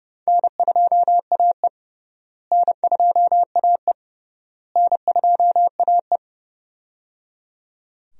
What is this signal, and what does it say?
Morse code, heard as a keyed tone, about 30 wpm, 700 Hz. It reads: N2AE N2AE N2AE
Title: USA Call Signs - 4 Characters - Repeated 3x - 30wpm